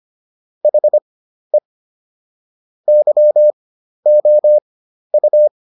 Morse code, heard as a keyed tone, about 25 wpm, spaced as for 12 wpm, 600 Hz. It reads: HE YOU